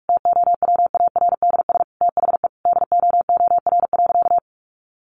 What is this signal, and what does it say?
Morse code, heard as a keyed tone, about 44 wpm, 700 Hz. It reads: TOWARDS THE DOOR.
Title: Sentences from Top 1200 Words 45wpm